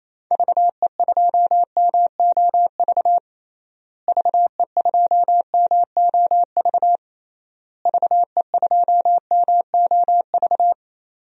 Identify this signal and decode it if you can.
Morse code, heard as a keyed tone, about 28 wpm, 700 Hz. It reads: VE2MOV VE2MOV VE2MOV
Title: DXCC-100 Call Signs - 6 Characters - Repeated 3x - 28wpm